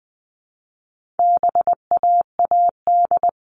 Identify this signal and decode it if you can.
Morse code, heard as a keyed tone, about 20 wpm, 700 Hz. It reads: BAAD
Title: All Cut Numbers - 4-Digit 20wpm